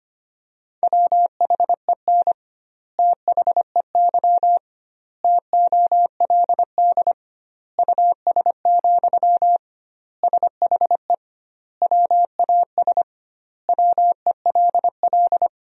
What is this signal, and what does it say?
Morse code, heard as a keyed tone, about 25 wpm, 700 Hz. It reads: WHEN THEY TOLD US, SHE WAS WELL
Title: 2x Word-Spacing - Sentences from Top 400 Words - 25wpm